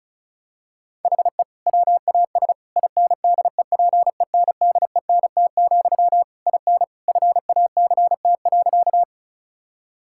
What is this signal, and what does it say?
Morse code, heard as a keyed tone, about 34 wpm, 700 Hz. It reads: HE WAS INDEPENDENT, IN FACT.
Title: Sentences from Top 1000 Words - Review Part 6 - 35wpm